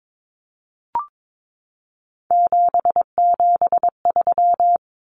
Morse code, heard as a keyed tone, about 22 wpm, 700 Hz. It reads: 773
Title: Common Cut Numbers - 3-Digit 22wpm